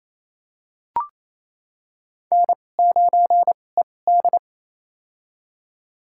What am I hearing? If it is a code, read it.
Morse code, heard as a keyed tone, about 28 wpm, spaced as for 20 wpm, 700 Hz. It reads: N9ED
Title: USA Call Signs - 4 Characters - 2x Character Spacing - 28wpm